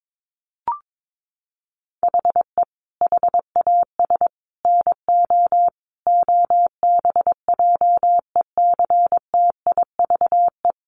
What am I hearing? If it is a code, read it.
Morse code, heard as a keyed tone, about 22 wpm, 700 Hz. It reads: HE HAS NO OBJECTIVE
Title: Sentences from Top 1200 Words 22wpm